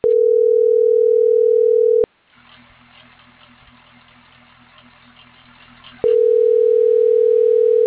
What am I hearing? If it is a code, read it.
Ringback tone (North America), 440 + 480 Hz.